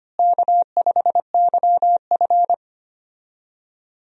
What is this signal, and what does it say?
Morse code, heard as a keyed tone, about 25 wpm, 700 Hz. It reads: K5YF
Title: USA Call Signs - 4 Characters - Repeated 2x - 25wpm